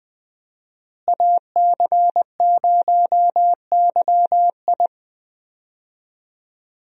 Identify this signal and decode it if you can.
Morse code, heard as a keyed tone, about 20 wpm, 700 Hz. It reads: AC0YI
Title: USA Call Signs - 5 Characters - Repeated 2x - 20wpm